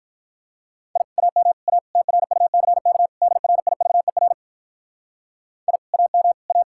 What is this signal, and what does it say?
Morse code, heard as a keyed tone, about 44 wpm, 700 Hz. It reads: I AM A TRUCK DRIVER I AM A
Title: Speed-Racing Common QSO Phrases - 35wpm